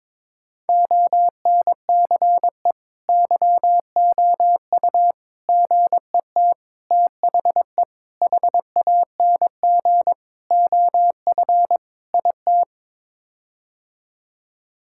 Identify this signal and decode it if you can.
Morse code, heard as a keyed tone, about 22 wpm, 700 Hz. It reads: ONCE YOU GET THE HANG OF IT
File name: practice / 22wpm / Sets of 7 Words - 22wpm